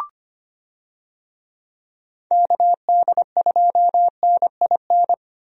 Morse code, heard as a keyed tone, about 25 wpm, 700 Hz. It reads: KD2NIN
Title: USA Call Signs - 6 Characters - Repeated 3x - 25wpm